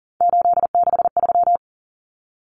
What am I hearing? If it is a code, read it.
Morse code, heard as a keyed tone, about 40 wpm, 700 Hz. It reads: 863
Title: Common Cut Numbers - 3-Digit 40wpm